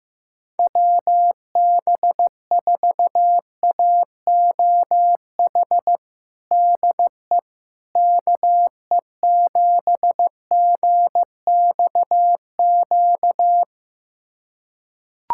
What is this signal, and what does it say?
Morse code, heard as a keyed tone, about 15 wpm, 700 Hz. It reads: WB4AOH DE KE7GXQ
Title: USA Callsign DE Callsign 15wpm